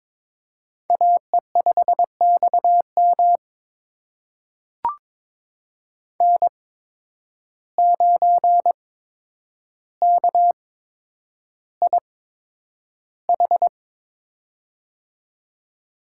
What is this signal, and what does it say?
Morse code, heard as a keyed tone, about 22 wpm, spaced as for 9 wpm, 700 Hz. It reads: AE5XM N9KIH
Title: USA Call Signs - 5 Characters - 8x Character Spacing - 22wpm